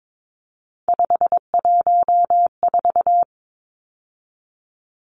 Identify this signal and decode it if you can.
Morse code, heard as a keyed tone, about 22 wpm, 700 Hz. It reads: 514
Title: Numbers - 3-Digit 22wpm